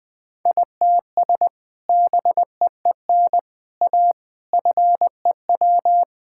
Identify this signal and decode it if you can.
Morse code, heard as a keyed tone, about 21 wpm, 700 Hz. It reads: ITS BEEN A FEW
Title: Sentences with 3-Word Phrases - 20wpm